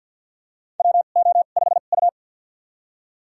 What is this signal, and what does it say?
Morse code, heard as a keyed tone, about 47 wpm, spaced as for 35 wpm, 700 Hz. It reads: WO5U
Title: USA Call Signs - 4 Characters - 2x Character Spacing - 50wpm